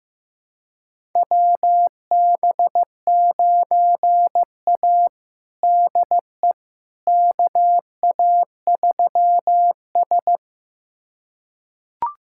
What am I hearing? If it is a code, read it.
Morse code, heard as a keyed tone, about 16 wpm, 700 Hz. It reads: WB9A DE KA3S
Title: USA Callsign DE Callsign - 4 Characters - 15wpm